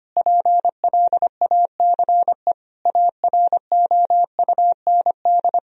Morse code, heard as a keyed tone, about 25 wpm, 700 Hz. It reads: PLACE AROUND
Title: Sentences from Top 1000 Words - Review Part 3 - 25wpm